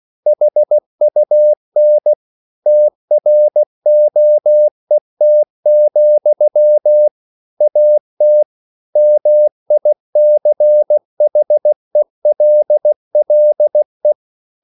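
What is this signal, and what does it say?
Morse code, heard as a keyed tone, about 16 wpm, 600 Hz. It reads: HUN TROET, AT MICHELLE
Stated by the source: Slow (QRS) Morse code at 16 wpm, from CW med Gnister.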